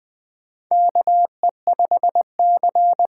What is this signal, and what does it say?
Morse code, heard as a keyed tone, about 20 wpm, 700 Hz. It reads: KE5C
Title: USA Call Signs - 6 Characters - 6x Character Spacing - 20wpm